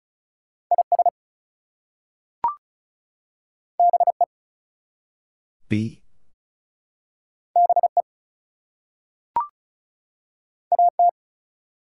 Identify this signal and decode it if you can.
Morse code, heard as a keyed tone, about 36 wpm, 700 Hz. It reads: IS BE BE AT
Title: Two Letter Words - 35wpm